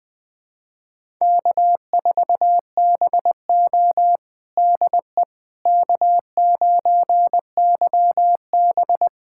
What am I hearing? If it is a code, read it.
Morse code, heard as a keyed tone, about 20 wpm, 700 Hz. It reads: K4BO DE K9YB
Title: USA Callsign DE Callsign - 4 Characters - 20wpm